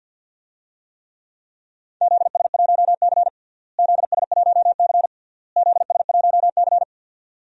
Morse code, heard as a keyed tone, about 50 wpm, 700 Hz. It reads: ZS1C ZS1C ZS1C
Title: DXCC-100 Call Signs - 4 Characters - Repeated 3x - 50wpm